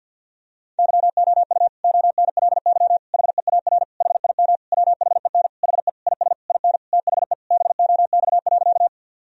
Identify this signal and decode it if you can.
Morse code, heard as a keyed tone, about 50 wpm, 700 Hz. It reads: YOU ONLY HEAR HIM WHEN HE IS IN THE BOX.
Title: Sentences from Top 1000 Words - Review Part 2 - 50wpm